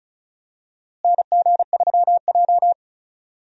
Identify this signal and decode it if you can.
Morse code, heard as a keyed tone, about 36 wpm, 700 Hz. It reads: NG3J
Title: USA Call Signs - 4 Characters - Repeated 2x - 35wpm